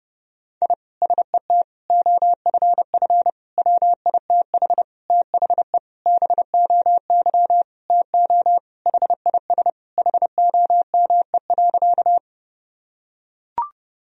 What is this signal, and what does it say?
Morse code, heard as a keyed tone, about 30 wpm, 700 Hz. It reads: I SET OFF WITH THE BOY TO HIS HOME.
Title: Sentences from Top 1000 Words - Review Part 1 - 30wpm